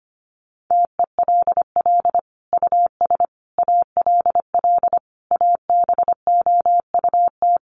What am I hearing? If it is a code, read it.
Morse code, heard as a keyed tone, about 25 wpm, 700 Hz. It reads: TELL US ALL ABOUT
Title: Sentences from Top 200 Words 25wpm